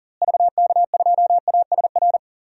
Morse code, heard as a keyed tone, about 40 wpm, 700 Hz. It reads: VK2ASR
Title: DXCC-100 Call Signs - 6 Characters - Repeated 2x - 40wpm